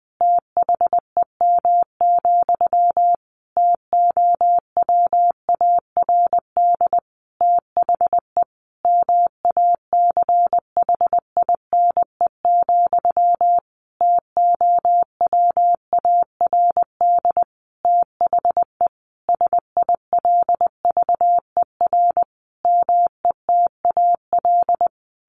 Morse code, heard as a keyed tone, about 20 wpm, 700 Hz. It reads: THEM, TOWARD THE MACHINE, TOWARD THE SILVER METAL
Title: A Sound Of Thunder 20wpm